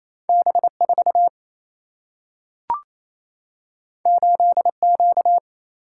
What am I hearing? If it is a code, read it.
Morse code, heard as a keyed tone, about 28 wpm, 700 Hz. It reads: B 4 8 Q
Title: Single-Letter Single-Number 28wpm